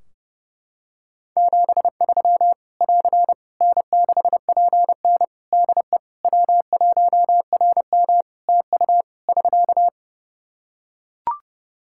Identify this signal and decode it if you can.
Morse code, heard as a keyed tone, about 28 wpm, 700 Hz. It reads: 73 <AR> N6PN DE W1RM TU <SK>
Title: Practice QSO - Exchange 4 - Ending - 30wpm